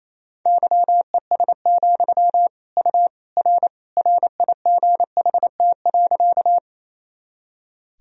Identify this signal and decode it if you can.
Morse code, heard as a keyed tone, about 28 wpm, 700 Hz. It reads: YES, U R RIGHT.
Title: Sentences from Top 500 Words and Abbreviations 28wpm